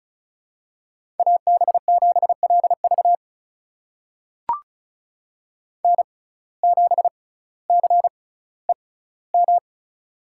Morse code, heard as a keyed tone, about 36 wpm, spaced as for 18 wpm, 700 Hz. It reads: AB7LV N7CEM
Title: USA Call Signs - 5 Characters - 6x Character Spacing - 35wpm